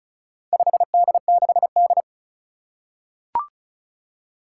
Morse code, heard as a keyed tone, about 36 wpm, 700 Hz. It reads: 5D6D
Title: Four Challenging Letter-Number Combinations 35wpm